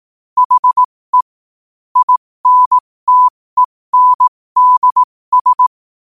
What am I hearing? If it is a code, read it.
Morse code, heard as a keyed tone, about 18 wpm, spaced as for 15 wpm, 1000 Hz. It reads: HE INTENDS
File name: qotdfiles / 15wpm